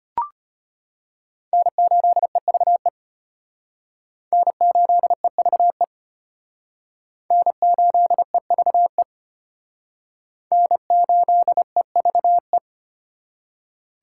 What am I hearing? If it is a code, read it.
Morse code, heard as a keyed tone, about 32 wpm, 700 Hz. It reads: N8EVE N8EVE N8EVE N8EVE